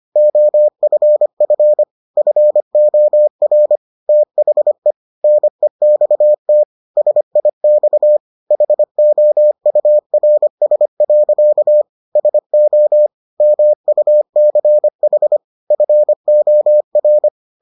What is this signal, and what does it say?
Morse code, heard as a keyed tone, about 25 wpm, 600 Hz. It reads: OFF FOR THE NEXT SIX HOURS. SO MUCH FOR